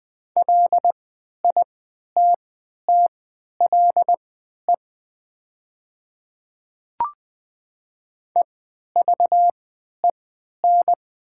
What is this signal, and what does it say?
Morse code, heard as a keyed tone, about 20 wpm, spaced as for 11 wpm, 700 Hz. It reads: LITTLE EVEN
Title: 3x Character Spacing - Top 200 Words - 20wpm